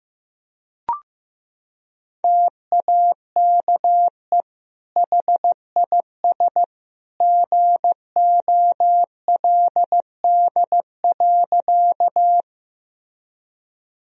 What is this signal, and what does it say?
Morse code, heard as a keyed tone, about 15 wpm, 700 Hz. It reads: TAKE HIS GOLD.
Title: Sentences from Top 1000 Words - Review Part 3 - 15wpm